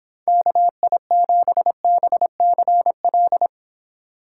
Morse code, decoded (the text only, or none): KI7BCL